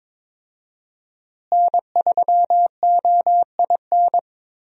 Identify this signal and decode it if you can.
Morse code, heard as a keyed tone, about 22 wpm, 700 Hz. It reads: N3OIN
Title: USA Call Signs - 5 Characters - Repeated 3x - 22wpm